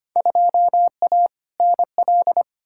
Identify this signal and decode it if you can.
Morse code, heard as a keyed tone, about 25 wpm, 700 Hz. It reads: 2A NL
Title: Field Day Exchanges Encore - 25wpm